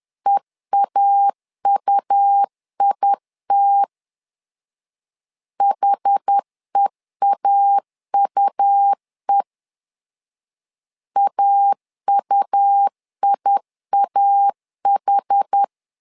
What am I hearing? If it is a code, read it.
Morse code, heard as a keyed tone, about 10 wpm, 800 Hz. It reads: EAUIT HEAUE AUIAH
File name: LEZIONI / 2 UHDN / 2 UH 10wpm 1714